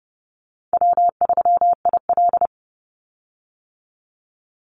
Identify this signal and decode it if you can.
Morse code, heard as a keyed tone, about 30 wpm, 700 Hz. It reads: W3IL
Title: Easy Call Signs - 30wpm